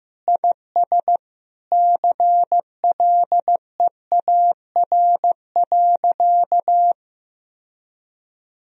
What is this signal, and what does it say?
Morse code, heard as a keyed tone, about 16 wpm, 700 Hz. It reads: IS CLEAR.
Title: Sentences from Top 1000 Words - Review Part 4 - 15wpm